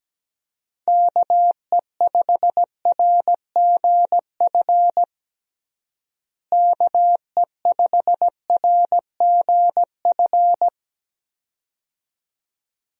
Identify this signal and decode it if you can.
Morse code, heard as a keyed tone, about 17 wpm, 700 Hz. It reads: KE5RGF KE5RGF
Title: USA Call Signs - 6 Characters - Repeated 2x - 17wpm